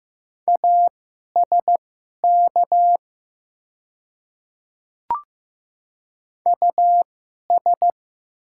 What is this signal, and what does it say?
Morse code, heard as a keyed tone, about 15 wpm, spaced as for 11 wpm, 700 Hz. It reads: ASK US
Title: 2x Character Spacing - Top 300 Words - 15wpm